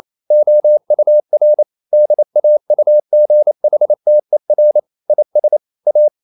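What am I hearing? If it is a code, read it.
Morse code, heard as a keyed tone, about 28 wpm, 600 Hz. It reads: OUR DAUGHTER IS A
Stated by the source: Morse code practice at 28 wpm, text of Ken Follett's novel Never in English, from CW med Gnister.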